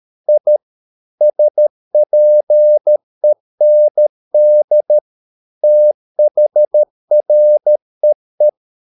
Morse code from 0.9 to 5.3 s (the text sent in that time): SPEND